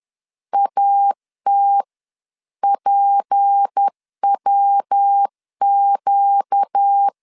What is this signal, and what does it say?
Morse code, heard as a keyed tone, about 10 wpm, 800 Hz. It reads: AT PWQ
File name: LEZIONI / 6 PKX0 / 6 PKX0 10wpm 2707